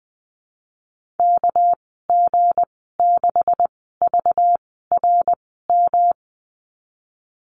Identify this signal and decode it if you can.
Morse code, heard as a keyed tone, about 20 wpm, spaced as for 15 wpm, 700 Hz. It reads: KG6VRM
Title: 231 Lesson 37g - Letter x - Call Signs 20wpm